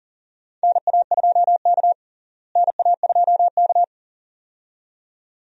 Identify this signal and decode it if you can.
Morse code, heard as a keyed tone, about 40 wpm, 700 Hz. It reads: NA2K NA2K
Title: USA Call Signs - 4 Characters - Repeated 2x - 40wpm